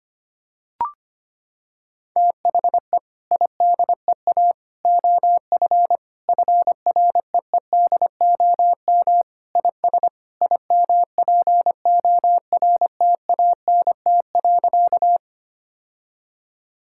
Morse code, heard as a keyed tone, about 25 wpm, 700 Hz. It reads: THE IDEA OF FREEDOM IS IMPORTANT.
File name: Sentences with 3-Word Phrases - 25wpm